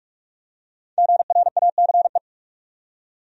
Morse code, heard as a keyed tone, about 46 wpm, 700 Hz. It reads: GRACE
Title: USA Common Names - Max 5 Letters - 45wpm